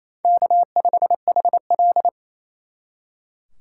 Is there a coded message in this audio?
Yes: Morse code from 0.2 to 2.1 s, heard as a keyed tone, about 36 wpm, 700 Hz. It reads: K5HL